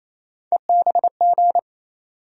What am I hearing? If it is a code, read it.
Morse code, heard as a keyed tone, about 28 wpm, 700 Hz. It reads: EBG
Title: USA Call Sign Postfixes - 3 Letters 28wpm